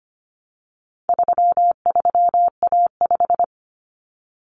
Morse code, heard as a keyed tone, about 25 wpm, 700 Hz. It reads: 33A5
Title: Common Cut Numbers - 4-Digit 25wpm